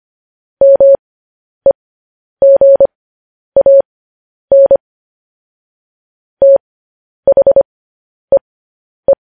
Morse code, heard as a keyed tone, about 25 wpm, spaced as for 10 wpm, 550 Hz. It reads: MEGAN THEE
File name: podcast / News headlines-1x-25x10